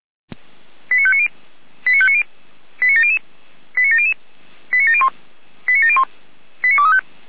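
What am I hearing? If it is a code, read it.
Five-tone selective call (ZVEI), heard as quick runs of five tones, tones of about 70 ms. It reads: 89590 89590 89700 89700 89701 89701 89235
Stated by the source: ZVEI five-tone call 89020 received on an SDR.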